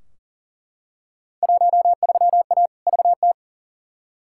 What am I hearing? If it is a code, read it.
Morse code, heard as a keyed tone, about 40 wpm, 700 Hz. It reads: 13A VT
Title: Field Day Exchanges Encore - 40wpm